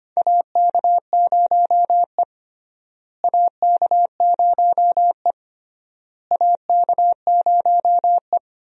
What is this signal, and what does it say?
Morse code, heard as a keyed tone, about 25 wpm, 700 Hz. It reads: AK0E AK0E AK0E
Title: USA Call Signs - 4 Characters - Repeated 3x - 25wpm